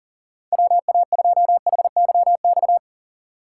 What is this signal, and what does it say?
Morse code, heard as a keyed tone, about 37 wpm, 700 Hz. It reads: WA2HYX
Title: USA Call Signs - 6 Characters - 2x Character Spacing - 40wpm